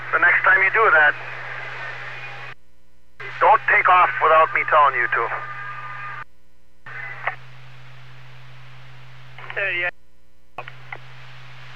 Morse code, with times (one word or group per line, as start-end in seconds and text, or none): none